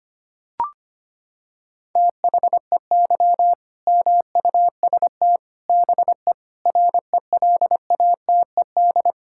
Morse code, heard as a keyed tone, about 25 wpm, 700 Hz. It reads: THEY MUST BE RELATED
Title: Sentences from Top 1000 Words - Review Part 5 - 25wpm